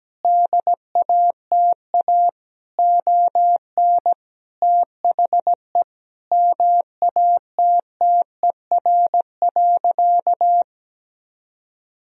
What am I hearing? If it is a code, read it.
Morse code, heard as a keyed tone, about 17 wpm, 700 Hz. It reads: DATA ON THE MATTER.
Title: Sentences from Top 1000 Words - Review Part 4 - 17wpm